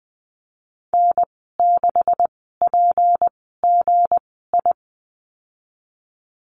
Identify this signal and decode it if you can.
Morse code, heard as a keyed tone, about 20 wpm, spaced as for 14 wpm, 700 Hz. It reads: N6PGI
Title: 245 Lesson 39g - Letter z - Call Signs 20wpm